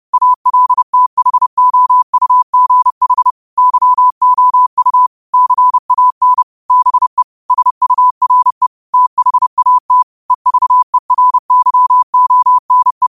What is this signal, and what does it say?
Morse code, heard as a keyed tone, about 30 wpm, 1000 Hz. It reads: ALTHOUGH YOU CAN BE SURE THAT EVERYONE